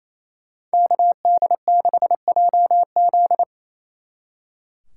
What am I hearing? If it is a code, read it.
Morse code, heard as a keyed tone, about 37 wpm, 700 Hz. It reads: KD6JZ